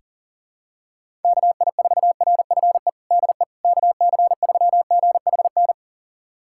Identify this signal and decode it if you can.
Morse code, heard as a keyed tone, about 40 wpm, 700 Hz. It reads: KI4RFE DE KC3GHN
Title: USA Callsign DE Callsign - 6 Characters - 40wpm